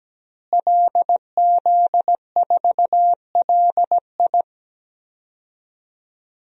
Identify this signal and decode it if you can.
Morse code, heard as a keyed tone, about 17 wpm, 700 Hz. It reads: LZ4LI
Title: DXCC-100 Call Signs - 5 Characters - Repeated 3x - 17wpm